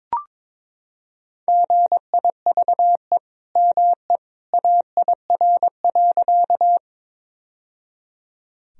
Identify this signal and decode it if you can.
Morse code, heard as a keyed tone, about 22 wpm, 700 Hz. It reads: GIVE ME AIR.
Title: Sentences from Top 1000 Words - Review Part 1 - 22wpm